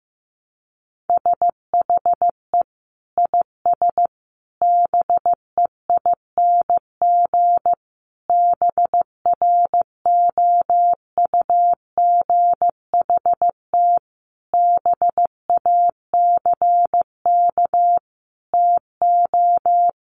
Morse code, heard as a keyed tone, about 15 wpm, 700 Hz. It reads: SHE IS BEING BROUGHT BACK TO
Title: Sentences from Top 600 Words 15wpm